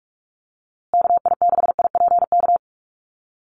Morse code, heard as a keyed tone, about 44 wpm, 690 Hz. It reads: KI6IPK
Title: USA Call Signs Repeated 2x 45wpm